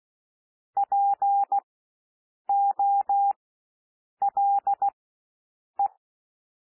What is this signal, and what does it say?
Morse code, heard as a keyed tone, about 16 wpm, spaced as for 7 wpm, 800 Hz. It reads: POLE
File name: cwmp3 / W1 SOUND WORDS 16 wpm